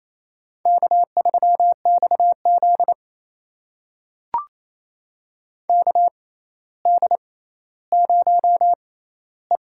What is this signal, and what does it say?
Morse code, heard as a keyed tone, about 28 wpm, spaced as for 14 wpm, 700 Hz. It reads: K3XZ KD0E
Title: USA Call Signs - 4 Characters - 6x Character Spacing - 28wpm